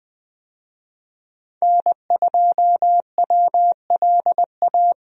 Morse code, heard as a keyed tone, about 20 wpm, 700 Hz. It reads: N2WLA
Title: USA Call Signs - 5 Characters - Repeated 2x - 20wpm